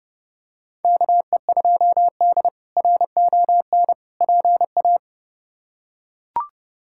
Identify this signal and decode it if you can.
Morse code, heard as a keyed tone, about 30 wpm, 700 Hz. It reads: KE2D RON PA
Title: K1USN SST QSO Practice - 30wpm